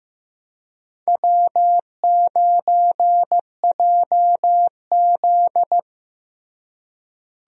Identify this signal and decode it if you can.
Morse code, heard as a keyed tone, about 15 wpm, 700 Hz. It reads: W9JZ
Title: USA Call Signs - 4 Characters - Repeated 2x - 15wpm